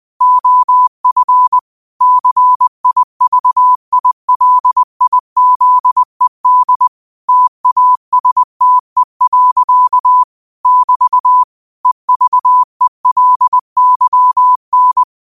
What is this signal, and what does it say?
Morse code, heard as a keyed tone, about 20 wpm, 1000 Hz. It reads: OF CIVILIZED TASTE. <BT> EVELYN
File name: qotdfiles / 20wpm